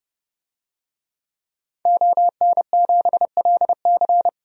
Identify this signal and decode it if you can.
Morse code, heard as a keyed tone, about 30 wpm, 700 Hz. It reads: ON7LC
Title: DXCC-100 Call Signs - 5 Characters - 30wpm